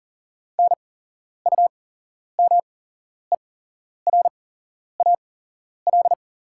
Morse code, heard as a keyed tone, about 40 wpm, spaced as for 11 wpm, 700 Hz. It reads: NUMERAL
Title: 8x Character Spacing - Top 500 Words - 40wpm